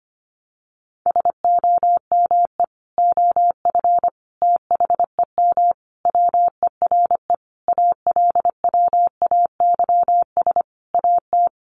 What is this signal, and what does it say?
Morse code, heard as a keyed tone, about 25 wpm, 700 Hz. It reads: SOME OF THEM WERE ALWAYS AT
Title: Sentences from Top 1200 Words 25wpm